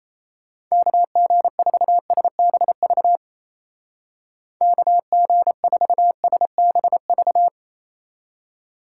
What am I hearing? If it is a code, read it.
Morse code, heard as a keyed tone, about 36 wpm, 700 Hz. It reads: KG4SBV KG4SBV